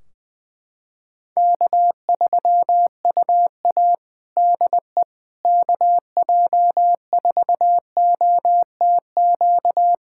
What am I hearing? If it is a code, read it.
Morse code, heard as a keyed tone, about 20 wpm, 700 Hz. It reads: K3UA DE KJ4OTQ
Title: Practice QSO - Exchange 4 - Ending - 20wpm